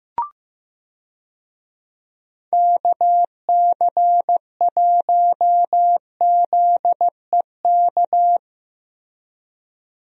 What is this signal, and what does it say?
Morse code, heard as a keyed tone, about 15 wpm, 700 Hz. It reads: KC1ZEK
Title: USA Call Signs - 6 Characters - Repeated 3x - 15wpm